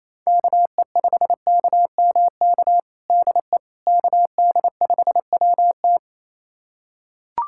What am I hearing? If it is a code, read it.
Morse code, heard as a keyed tone, about 28 wpm, 700 Hz. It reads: KE5KMK DE KD5WT
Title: USA Callsign DE Callsign 28wpm